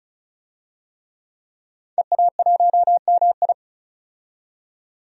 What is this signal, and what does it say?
Morse code, heard as a keyed tone, about 36 wpm, 700 Hz. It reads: EA1MI
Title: DXCC-100 Call Signs - 5 Characters - 35wpm